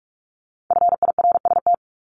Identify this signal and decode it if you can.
Morse code, heard as a keyed tone, about 46 wpm, 700 Hz. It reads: FIRST
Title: Multi-letter Phonemes Repeated 3x 45wpm